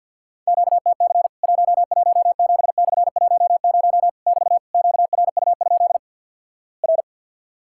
Morse code, heard as a keyed tone, about 50 wpm, 700 Hz. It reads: QTC 117/10 <BT> QRU? R
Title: Worked All Europe DX Contest - QTC Exchange with Repeats - 50wpm